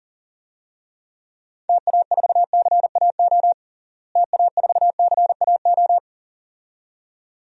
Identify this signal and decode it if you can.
Morse code, heard as a keyed tone, about 40 wpm, 700 Hz. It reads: TA4CAO TA4CAO
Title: DXCC-100 Call Signs - 6 Characters - Repeated 2x - 40wpm